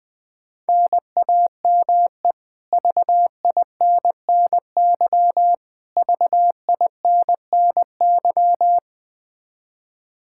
Morse code, heard as a keyed tone, about 20 wpm, 700 Hz. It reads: NAME VINNY VINNY
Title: Practice QSO - Exchange 1 - Beginning - 20wpm